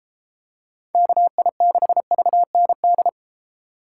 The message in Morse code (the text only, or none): KI6VND